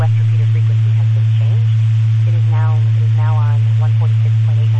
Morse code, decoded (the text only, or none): none